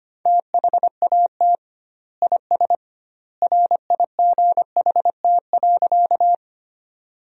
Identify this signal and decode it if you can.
Morse code, heard as a keyed tone, about 25 wpm, 700 Hz. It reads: THAT IS RIGHT.
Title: 2x Word-Spacing - Sentences from Top 200 Words - 25wpm